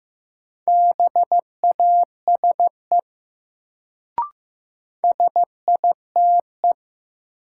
Morse code, heard as a keyed tone, about 15 wpm, 700 Hz. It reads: BASE SITE
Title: Sentences from Top 1000 Words - Review Part 1 - New Word Drill - 15wpm